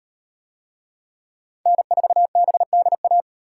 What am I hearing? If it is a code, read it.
Morse code, heard as a keyed tone, about 38 wpm, 700 Hz. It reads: N4BDA